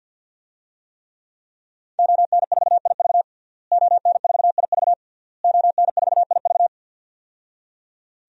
Morse code, heard as a keyed tone, about 52 wpm, 700 Hz. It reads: ON4IV ON4IV ON4IV
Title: DXCC-100 Call Signs - 5 Characters - Repeated 3x - 50wpm